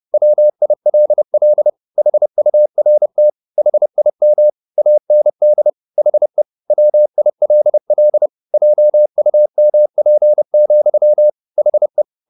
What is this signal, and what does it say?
Morse code, heard as a keyed tone, about 30 wpm, 600 Hz. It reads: WILL HURT HIM AND HE WILL JUMP, HE
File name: the old man and the sea / the old man and the sea 1 30wpm